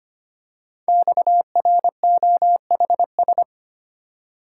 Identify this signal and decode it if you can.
Morse code, heard as a keyed tone, about 25 wpm, 700 Hz. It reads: XROHS
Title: Code Groups - 5 Letters - 25wpm